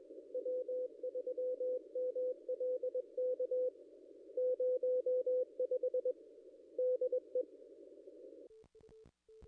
Morse code, heard as a keyed tone, about 21 wpm, 500 Hz. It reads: W3MLK 05 DE
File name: contests / ss-cw-k5tr-op-n5rz -0100